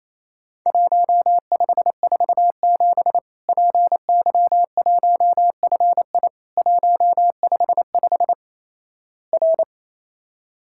Morse code, heard as a keyed tone, about 28 wpm, 700 Hz. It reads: 1547 PY1FI 155 R
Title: Worked All Europe DX Contest - QTC Exchange - 28wpm